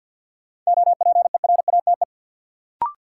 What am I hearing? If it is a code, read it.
Morse code, heard as a keyed tone, about 53 wpm, 700 Hz. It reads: OPERATE
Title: Sentences from Top 1500 Words - New Word Drill - 50wpm